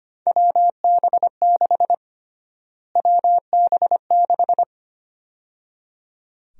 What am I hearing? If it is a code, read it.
Morse code, heard as a keyed tone, about 25 wpm, 700 Hz. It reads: WB6 WB6
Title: USA Call Sign Prefixes - Repeated 2x - 25wpm